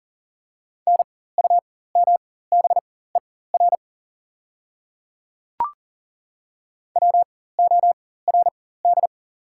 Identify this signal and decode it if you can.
Morse code, heard as a keyed tone, about 40 wpm, spaced as for 19 wpm, 700 Hz. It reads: NUMBER WORD
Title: 4x Character Spacing - Top 100 Words - 40wpm